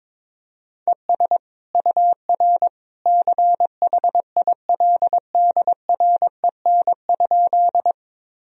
Morse code, heard as a keyed tone, about 22 wpm, 700 Hz. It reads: ES UR CHILDREN?
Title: Sentences from Top 300 Words and Abbreviations 22wpm